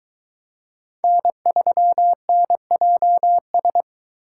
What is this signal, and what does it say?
Morse code, heard as a keyed tone, about 21 wpm, 700 Hz. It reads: N3NJS